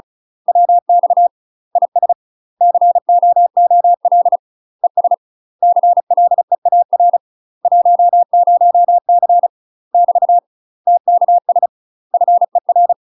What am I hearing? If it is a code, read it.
Morse code, heard as a keyed tone, about 36 wpm, 700 Hz. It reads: WX IS COOL ES CLEAR 10C <BT> TKS FER
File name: UK MASTER QSOs TRAINING / 35WPM / UK MASTER QSOs 0033